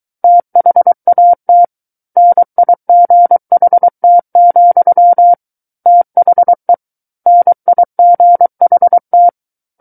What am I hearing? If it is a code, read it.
Morse code, heard as a keyed tone, about 23 wpm, 700 Hz. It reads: THAT NIGHT, THE NIGHT
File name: Book One / 14-Chapter 14 - 23 WPM